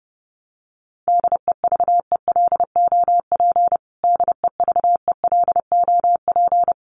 Morse code, heard as a keyed tone, about 30 wpm, 700 Hz. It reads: DEVELOP DEVELOP
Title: Top 500 Words Repeated 30wpm